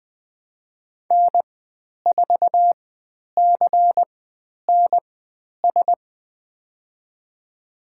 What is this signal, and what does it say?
Morse code, heard as a keyed tone, about 20 wpm, spaced as for 10 wpm, 700 Hz. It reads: N4CNS
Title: 095 Lesson 16g - Letter c - Call Signs 20wpm